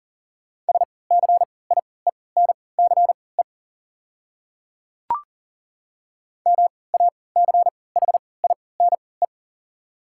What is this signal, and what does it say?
Morse code, heard as a keyed tone, about 40 wpm, spaced as for 23 wpm, 690 Hz. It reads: SCIENCE MACHINE
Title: 3x Character Spacing - Top 500 Words - 40wpm